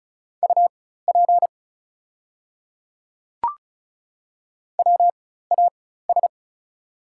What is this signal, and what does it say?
Morse code, heard as a keyed tone, about 36 wpm, spaced as for 16 wpm, 700 Hz. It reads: UP WAS
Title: 4x Character Spacing - Top 100 Words - 35wpm